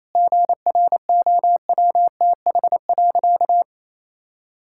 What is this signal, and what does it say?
Morse code, heard as a keyed tone, about 28 wpm, 700 Hz. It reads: GROWTH.
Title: Sentences from Top 1000 Words - Review Part 5 - 28wpm